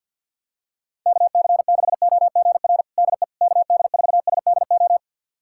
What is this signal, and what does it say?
Morse code, heard as a keyed tone, about 52 wpm, 690 Hz. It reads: KC6OGR DE KD4SDO
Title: USA Callsign DE Callsign 50wpm